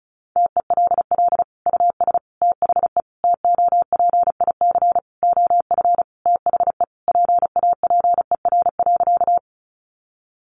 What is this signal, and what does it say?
Morse code, heard as a keyed tone, about 34 wpm, 700 Hz. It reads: TELL US THE TOPIC OF THE PAPER.
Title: Sentences from Top 1200 Words 35wpm